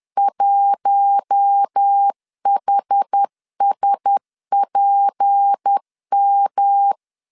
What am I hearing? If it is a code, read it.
Morse code, heard as a keyed tone, about 10 wpm, 800 Hz. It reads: 1HSPM
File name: LEZIONI / 6 PKX0 / 6 P 10wpm 2432